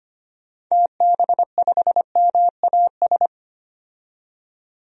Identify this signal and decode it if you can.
Morse code, heard as a keyed tone, about 25 wpm, 690 Hz. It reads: TB5MAS
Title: DXCC-100 Call Signs - 6 Characters - 25wpm